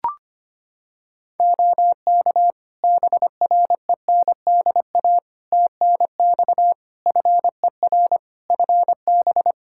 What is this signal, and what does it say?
Morse code, heard as a keyed tone, about 25 wpm, 700 Hz. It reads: OK BRENDA TNX FER FB
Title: Practice QSO - Exchange 4 - Ending - 25wpm